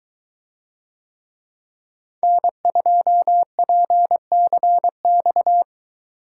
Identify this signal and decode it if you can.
Morse code, heard as a keyed tone, about 22 wpm, 700 Hz. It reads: N2PCX